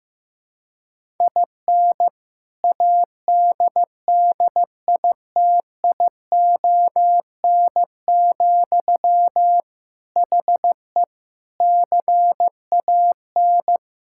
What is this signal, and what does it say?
Morse code, heard as a keyed tone, about 15 wpm, 700 Hz. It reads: IN ADDITION, HE CAN
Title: Sentences from Top 1000 Words - Review Part 6 - 15wpm